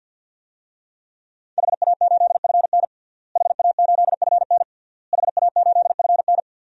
Morse code, heard as a keyed tone, about 52 wpm, 700 Hz. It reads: HA8FN HA8FN HA8FN
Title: DXCC-100 Call Signs - 5 Characters - Repeated 3x - 50wpm